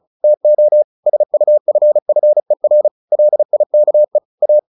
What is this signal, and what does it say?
Morse code, heard as a keyed tone, about 34 wpm, 600 Hz. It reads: TO SUFFER LIKE A
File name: the old man and the sea 2 35wpm